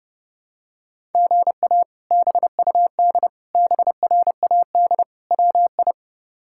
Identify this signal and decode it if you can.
Morse code, heard as a keyed tone, about 32 wpm, 700 Hz. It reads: GA BUD BRAD WI
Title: K1USN SST QSO Practice - 30wpm